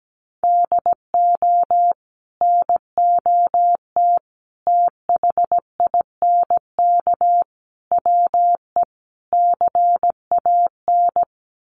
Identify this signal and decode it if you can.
Morse code, heard as a keyed tone, about 17 wpm, 700 Hz. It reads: DO NOT THINK WE CAN
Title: Sets of 5 Words Encore2 17wpm